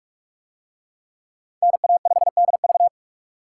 Morse code, heard as a keyed tone, about 43 wpm, 700 Hz. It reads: NA5DV